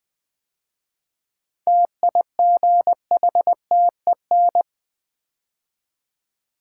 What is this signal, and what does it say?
Morse code, heard as a keyed tone, about 20 wpm, 700 Hz. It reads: TIGHTEN
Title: 7 Letter Words - First 3 Letters Given - 20wpm